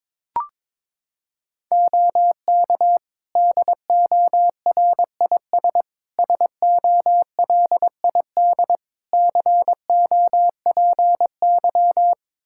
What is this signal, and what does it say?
Morse code, heard as a keyed tone, about 22 wpm, 700 Hz. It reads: OK DORIS SOLID COPY
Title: Practice QSO - Exchange 3 - More Info - 22wpm